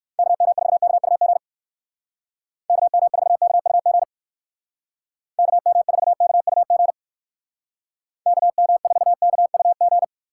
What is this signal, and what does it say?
Morse code, heard as a keyed tone, about 56 wpm, 700 Hz. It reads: KM4KUG KM4KUG KM4KUG KM4KUG
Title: Speed Racing - USA 2x3 Call Signs 45wpm